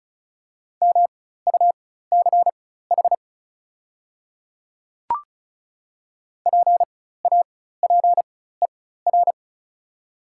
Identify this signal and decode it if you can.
Morse code, heard as a keyed tone, about 36 wpm, spaced as for 16 wpm, 700 Hz. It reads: MUCH PAPER
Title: 4x Character Spacing - Top 300 Words - 35wpm